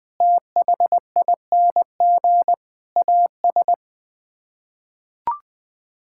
Morse code, heard as a keyed tone, about 20 wpm, 700 Hz. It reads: THING AS
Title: Sets of 6 Words - 20wpm